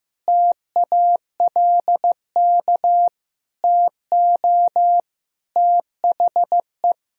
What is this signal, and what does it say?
Morse code, heard as a keyed tone, about 16 wpm, 700 Hz. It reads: TALK TO THE
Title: Sentences from Top 400 Words and Abbreviations 15wpm